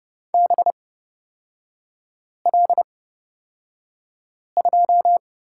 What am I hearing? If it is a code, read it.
Morse code, heard as a keyed tone, about 30 wpm, spaced as for 5 wpm, 700 Hz. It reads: BL2
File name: Lesson 36e - BT - Three Characters 30wpm